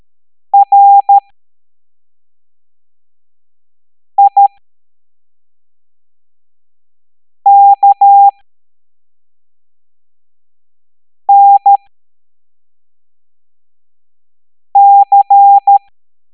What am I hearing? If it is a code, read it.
Morse code, heard as a keyed tone, about 13 wpm, spaced as for 3 wpm, 800 Hz. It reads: RIKNC